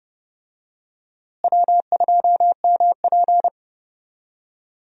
Morse code, heard as a keyed tone, about 30 wpm, 700 Hz. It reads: W2MP
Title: USA Call Signs - 4 Characters - Repeated 3x - 30wpm